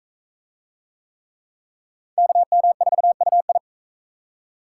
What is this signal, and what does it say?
Morse code, heard as a keyed tone, about 39 wpm, 700 Hz. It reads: KM4UI